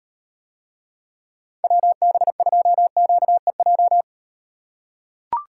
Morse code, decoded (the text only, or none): WB2QEJ